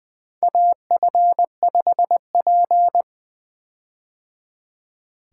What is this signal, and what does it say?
Morse code, heard as a keyed tone, about 20 wpm, 700 Hz. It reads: AF5P
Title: K1USN SST QSO Practice - 20wpm